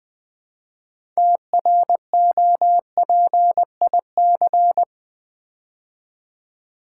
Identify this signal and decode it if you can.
Morse code, heard as a keyed tone, about 20 wpm, 700 Hz. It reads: TROPIC
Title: Build A Word - 3 Letters At A Time - 8-Letter Words - 20wpm